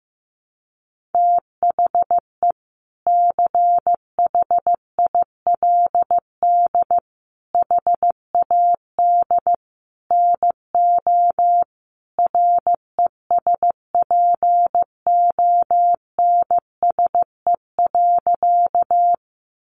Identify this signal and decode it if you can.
Morse code, heard as a keyed tone, about 15 wpm, 700 Hz. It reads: THE CHILD HAD NO RESPONSE.
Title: Sentences from Top 1400 Words 15wpm